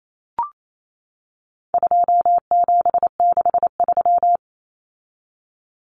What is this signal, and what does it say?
Morse code, heard as a keyed tone, about 28 wpm, 700 Hz. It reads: 2763
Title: Common Cut Numbers - 4-Digit 28wpm